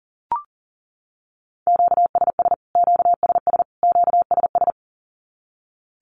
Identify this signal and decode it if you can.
Morse code, heard as a keyed tone, about 40 wpm, 700 Hz. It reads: QSS QSS QSS
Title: Q-Codes Repeated 3x 40wpm